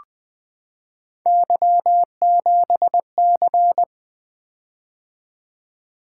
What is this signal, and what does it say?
Morse code, heard as a keyed tone, about 20 wpm, 700 Hz. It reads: Y7C
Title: Three Letter-Number Combinations - 20wpm